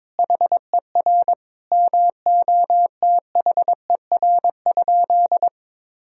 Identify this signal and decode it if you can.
Morse code, heard as a keyed tone, about 21 wpm, 700 Hz. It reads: HER MOTHER?
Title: Sentences from Top 1000 Words - Review Part 3 - 22wpm